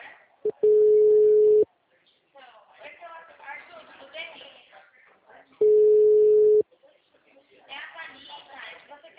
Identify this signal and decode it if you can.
Ringback tone (Europe, 425 Hz), with speech.